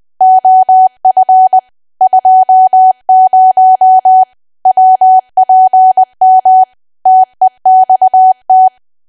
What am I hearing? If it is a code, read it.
Morse code, heard as a keyed tone, about 20 wpm, 750 Hz. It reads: OF 20 WPM TEXT
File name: cwa / W1AW-Level3 / 20WPM